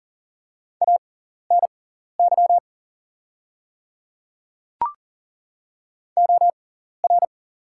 Morse code, heard as a keyed tone, about 40 wpm, spaced as for 14 wpm, 700 Hz. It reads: ANY OR
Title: 6x Character Spacing - Top 200 Words - 40wpm